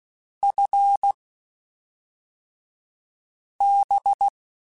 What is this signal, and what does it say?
Morse code, heard as a keyed tone, about 16 wpm, spaced as for 3 wpm, 800 Hz. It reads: FB
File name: QSOs / QSOs-mp3s / WD0FBT 1@5